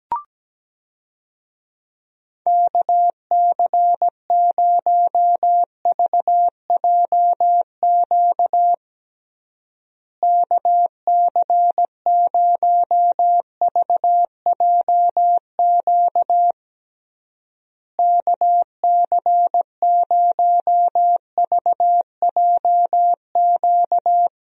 Morse code, heard as a keyed tone, about 17 wpm, 700 Hz. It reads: KC0VJQ KC0VJQ KC0VJQ
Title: USA Call Signs - 6 Characters - Repeated 3x - 17wpm